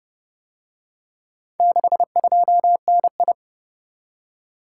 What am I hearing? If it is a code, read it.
Morse code, heard as a keyed tone, about 29 wpm, 700 Hz. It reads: 62NI